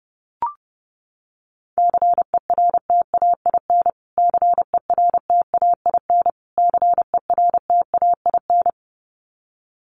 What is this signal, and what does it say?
Morse code, heard as a keyed tone, about 30 wpm, 700 Hz. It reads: CERTAIN CERTAIN CERTAIN
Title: Top 500 Words Repeated 30wpm